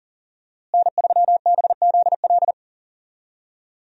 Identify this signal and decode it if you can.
Morse code, heard as a keyed tone, about 37 wpm, 700 Hz. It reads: N3BZL
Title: USA Call Signs - 5 Characters - Repeated 3x - 40wpm